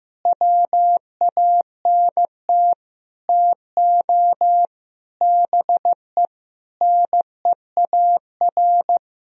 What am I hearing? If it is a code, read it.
Morse code, heard as a keyed tone, about 16 wpm, 700 Hz. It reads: WANT TO BE NEAR
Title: Sentences from Top 1000 Words - Review Part 1 - 15wpm